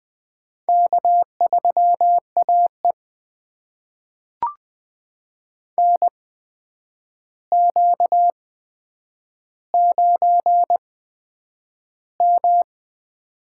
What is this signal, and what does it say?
Morse code, heard as a keyed tone, about 20 wpm, spaced as for 9 wpm, 700 Hz. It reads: K3AE NQ9M
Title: USA Call Signs - 8x Character Spacing - 20wpm